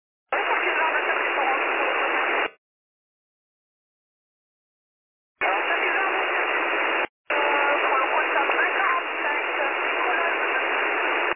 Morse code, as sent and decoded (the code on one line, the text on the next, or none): none